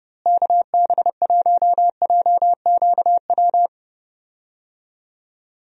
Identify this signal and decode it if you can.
Morse code, heard as a keyed tone, about 30 wpm, 700 Hz. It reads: KB1JQW
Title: USA Call Signs - 6 Characters - Repeated 3x - 30wpm